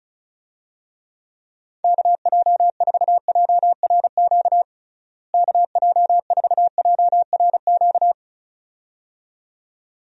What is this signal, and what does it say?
Morse code, heard as a keyed tone, about 34 wpm, 700 Hz. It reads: KJ4JRQ KJ4JRQ
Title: USA Call Signs - 6 Characters - Repeated 2x - 35wpm